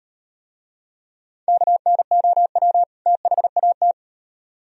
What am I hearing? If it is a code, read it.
Morse code, heard as a keyed tone, about 38 wpm, 700 Hz. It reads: KNOW THAT